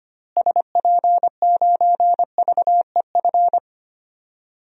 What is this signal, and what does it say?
Morse code, heard as a keyed tone, about 25 wpm, 700 Hz. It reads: SP9VEF
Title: DXCC-100 Call Signs - 6 Characters - Repeated 2x - 25wpm